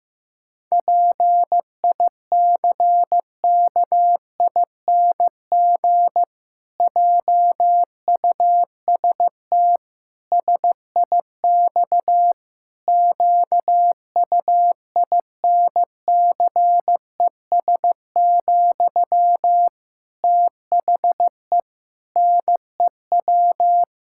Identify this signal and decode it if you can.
Morse code, heard as a keyed tone, about 15 wpm, 700 Hz. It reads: PICKING JUST SIX QUINCES, THE NEW
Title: Pangram Challenge v2 - 15wpm